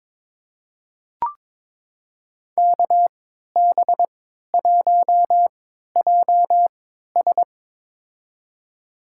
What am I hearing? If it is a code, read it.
Morse code, heard as a keyed tone, about 22 wpm, spaced as for 13 wpm, 700 Hz. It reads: KB1JS
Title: USA Call Signs - 5 Characters - 3x Character Spacing - 22wpm